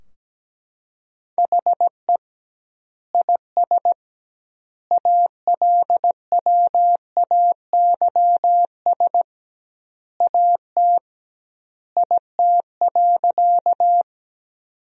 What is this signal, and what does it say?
Morse code, heard as a keyed tone, about 17 wpm, 700 Hz. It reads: HE IS ALWAYS AT IT.
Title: 2x Word-Spacing - Sentences from Top 300 Words - 17wpm